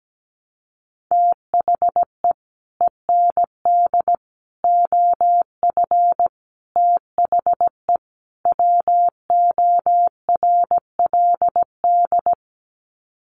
Morse code, heard as a keyed tone, about 17 wpm, 700 Hz. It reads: THE END OF THE WORLD
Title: Sets of 5 Words 17wpm